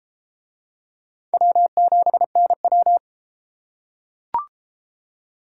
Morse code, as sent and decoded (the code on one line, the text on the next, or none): .-- --... -. .--
W7NW